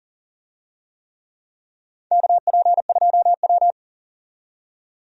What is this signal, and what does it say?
Morse code, heard as a keyed tone, about 40 wpm, 700 Hz. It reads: KP2W
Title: USA Call Signs - 4 Characters - 40wpm